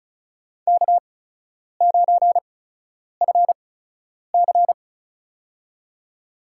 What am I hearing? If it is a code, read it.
Morse code, heard as a keyed tone, about 36 wpm, spaced as for 10 wpm, 700 Hz. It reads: K9FC
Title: USA Call Signs - 4 Characters - 8x Character Spacing - 35wpm